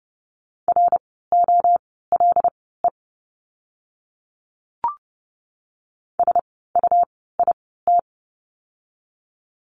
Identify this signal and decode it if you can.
Morse code, heard as a keyed tone, about 30 wpm, spaced as for 17 wpm, 700 Hz. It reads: ROLE SUIT
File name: Lesson 17f - Letter u - Words 30wpm